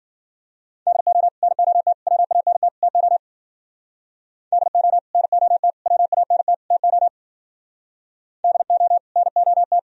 Morse code, heard as a keyed tone, about 50 wpm, 700 Hz. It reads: DO NOT WANT TO DO NOT WANT TO DO NOT
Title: Speed-Racing - Sets of 4 Words 40wpm